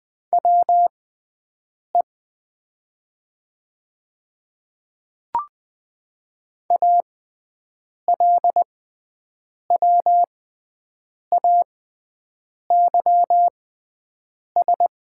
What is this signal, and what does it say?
Morse code, heard as a keyed tone, about 20 wpm, spaced as for 7 wpm, 700 Hz. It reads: WE ALWAYS
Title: 6x Character Spacing - Top 500 Words - 20wpm